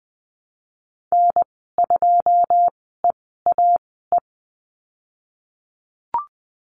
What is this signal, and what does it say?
Morse code, heard as a keyed tone, about 20 wpm, spaced as for 15 wpm, 700 Hz. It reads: N2EAE
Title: 128 Lesson 21g - Number 6 - Call Signs 20wpm